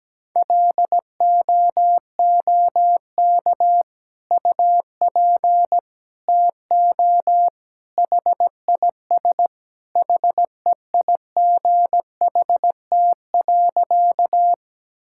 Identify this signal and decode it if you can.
Morse code, heard as a keyed tone, about 17 wpm, 700 Hz. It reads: LOOK UP TO HIS HEIGHT.
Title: Sentences from Top 1600 Words 17wpm